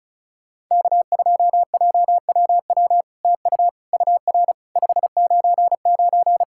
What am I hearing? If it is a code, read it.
Morse code, heard as a keyed tone, about 36 wpm, 700 Hz. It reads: K2JWW TU UR 599
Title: SOTA QSO Practice 35wpm